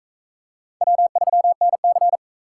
Morse code, heard as a keyed tone, about 44 wpm, 700 Hz. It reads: W3NC